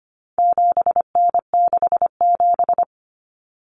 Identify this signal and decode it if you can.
Morse code, heard as a keyed tone, about 25 wpm, 700 Hz. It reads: 7N67
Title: Common Cut Numbers - 4-Digit 25wpm